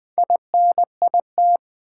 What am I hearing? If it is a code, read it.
Morse code, heard as a keyed tone, about 20 wpm, 700 Hz. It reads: INIT
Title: Build A Word - 2 Letters At A Time - 8-Letter Words - 20wpm